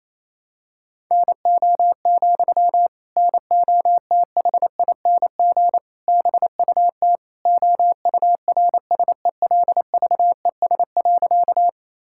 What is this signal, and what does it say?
Morse code, heard as a keyed tone, about 28 wpm, 700 Hz. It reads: NO, NOTHING BUT OURSELVES.
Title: Sentences from Top 1600 Words 28wpm